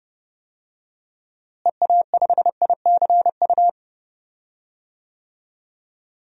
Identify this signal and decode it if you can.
Morse code, heard as a keyed tone, about 30 wpm, 700 Hz. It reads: EA5ICU
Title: DXCC-100 Call Signs - 6 Characters - Repeated 3x - 30wpm